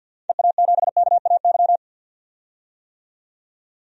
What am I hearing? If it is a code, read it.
Morse code, heard as a keyed tone, about 50 wpm, 700 Hz. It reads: EA7KAY
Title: DXCC-100 Call Signs - 6 Characters - 50wpm